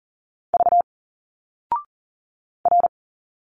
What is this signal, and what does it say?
Morse code, heard as a keyed tone, about 40 wpm, 700 Hz. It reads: V R
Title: Single Letters 40wpm